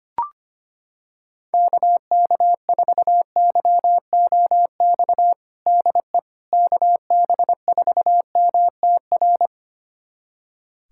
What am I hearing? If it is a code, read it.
Morse code, heard as a keyed tone, about 25 wpm, 700 Hz. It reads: KK4YOX DE KB4MTR
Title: USA Callsign DE Callsign 25wpm